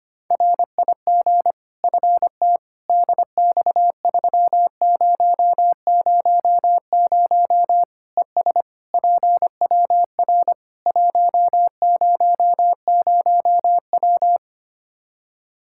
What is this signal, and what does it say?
Morse code, heard as a keyed tone, about 25 wpm, 700 Hz. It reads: RIG FT DX3000 ES PWR 100W
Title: Practice QSO - Exchange 2 - Info - 25wpm